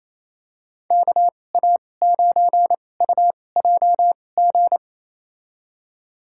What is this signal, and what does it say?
Morse code, heard as a keyed tone, about 28 wpm, spaced as for 20 wpm, 700 Hz. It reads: KA9UJG
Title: USA Call Signs - 6 Characters - 2x Character Spacing - 28wpm